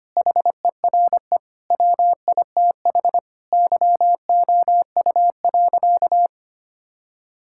Morse code, heard as a keyed tone, about 25 wpm, 700 Hz. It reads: HERE WITH YOU.
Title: Sentences from Top 1600 Words 25wpm